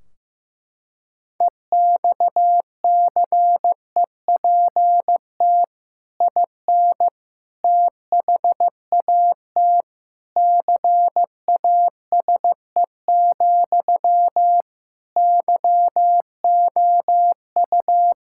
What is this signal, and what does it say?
Morse code, heard as a keyed tone, about 15 wpm, 700 Hz. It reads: EXCEPT IN THAT CASE, YOU
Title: Sentences from Top 1600 Words 15wpm